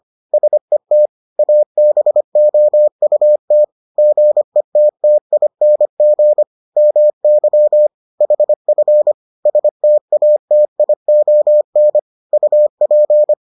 Morse code, heard as a keyed tone, about 25 wpm, 600 Hz. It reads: SET ABOUT GETTING MY HF STATION UP